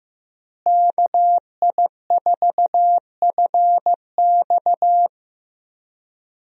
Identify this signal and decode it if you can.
Morse code, heard as a keyed tone, about 15 wpm, 700 Hz. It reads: KI4FX
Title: USA Call Signs - 5 Characters - Repeated 3x - 15wpm